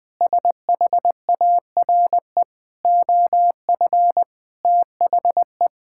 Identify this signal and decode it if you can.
Morse code, heard as a keyed tone, about 20 wpm, 700 Hz. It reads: SHARE OF THE
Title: Sentences from Top 1000 Words - Review Part 6 - 20wpm